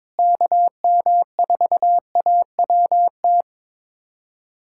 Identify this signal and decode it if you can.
Morse code, heard as a keyed tone, about 22 wpm, 700 Hz. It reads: KM4AWT
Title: USA Callsign DE Callsign - 6 Characters - 22wpm